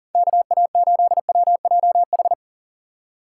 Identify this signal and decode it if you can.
Morse code, heard as a keyed tone, about 40 wpm, 700 Hz. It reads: KA8WJH
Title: USA Call Signs - 4x Character Spacing - 40wpm